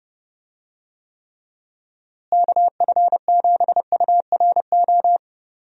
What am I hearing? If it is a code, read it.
Morse code, heard as a keyed tone, about 32 wpm, 700 Hz. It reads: KF7URO
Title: USA Call Signs - 6 Characters - Repeated 2x - 30wpm